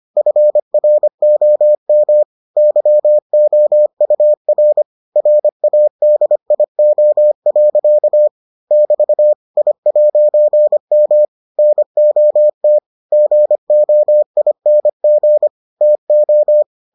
Morse code, heard as a keyed tone, about 25 wpm, 600 Hz. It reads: FROM YOUR RADIO. <BT> I'M NOT GOING TO